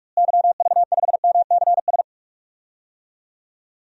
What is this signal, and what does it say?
Morse code, heard as a keyed tone, about 46 wpm, 700 Hz. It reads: YV5MCS
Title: DXCC-100 Call Signs - 6 Characters - Repeated 2x - 45wpm